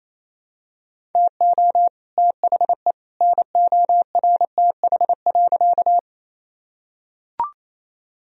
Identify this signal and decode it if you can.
Morse code, heard as a keyed tone, about 28 wpm, 700 Hz. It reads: TO THE NORTH.
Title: Sentences from Top 1000 Words - Review Part 1 - 28wpm